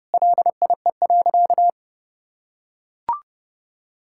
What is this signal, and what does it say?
Morse code, heard as a keyed tone, about 30 wpm, 700 Hz. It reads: LIE.
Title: Sentences from Top 1600 Words 30wpm